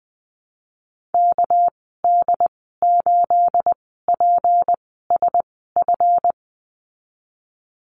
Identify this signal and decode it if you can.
Morse code, heard as a keyed tone, about 20 wpm, spaced as for 15 wpm, 700 Hz. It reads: KD8PSF
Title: 245 Lesson 39g - Letter z - Call Signs 20wpm